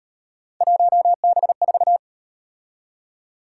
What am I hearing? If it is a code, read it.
Morse code, heard as a keyed tone, about 38 wpm, 700 Hz. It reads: 1B4